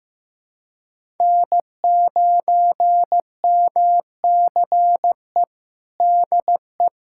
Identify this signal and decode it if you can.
Morse code, heard as a keyed tone, about 15 wpm, 700 Hz. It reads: N9MCE DE
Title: USA Callsign DE Callsign - 5 Characters - 15wpm